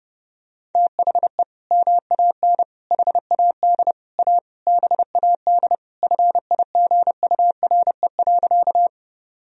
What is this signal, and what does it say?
Morse code, heard as a keyed tone, about 30 wpm, 700 Hz. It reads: THE MAN HAD A BAD FIGURE.
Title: Sentences from Top 1000 Words - Review Part 2 - 30wpm